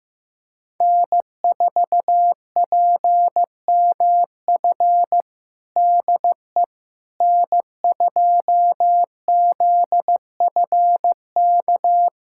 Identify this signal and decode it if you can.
Morse code, heard as a keyed tone, about 15 wpm, 700 Hz. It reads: N4PMF DE N2ZFK
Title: USA Callsign DE Callsign - 5 Characters - 15wpm